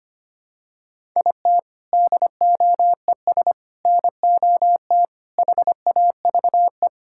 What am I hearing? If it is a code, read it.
Morse code, heard as a keyed tone, about 25 wpm, 700 Hz. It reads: IT DOES NOT HAVE
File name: practice / 25wpm / Sentences with 4-Word Phrases - 25wpm